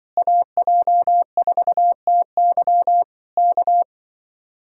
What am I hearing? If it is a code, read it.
Morse code, heard as a keyed tone, about 23 wpm, 700 Hz. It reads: AJ4TY K
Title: Speed-Racing Common QSO Phrases - 20wpm